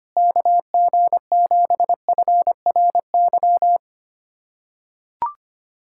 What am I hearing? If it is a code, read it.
Morse code, heard as a keyed tone, about 25 wpm, 700 Hz. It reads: KG7FRY
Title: USA Callsign DE Callsign - 6 Characters - 25wpm